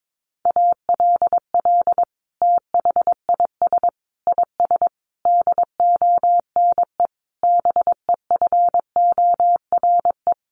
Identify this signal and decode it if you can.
Morse code, heard as a keyed tone, about 22 wpm, 700 Hz. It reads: ALL THIS IS DONE BEFORE
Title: Sentences from Top 1500 Words 22wpm